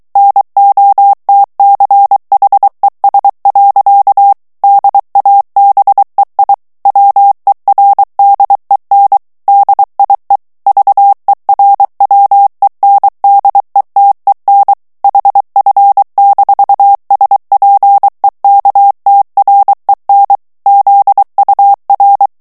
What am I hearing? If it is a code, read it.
Morse code, heard as a keyed tone, about 23 wpm, 800 Hz. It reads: NOTCHES. DABEI WERDEN DIE VERWENDETEN HF-SPEKTREN ZUR
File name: morse mp3 / ALLEMAND texte clair 1 vitesse 22 mots minute